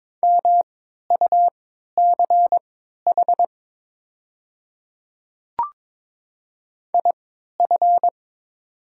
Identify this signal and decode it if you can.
Morse code, heard as a keyed tone, about 22 wpm, spaced as for 13 wpm, 690 Hz. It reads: MUCH IF
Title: 3x Character Spacing - Top 200 Words - 22wpm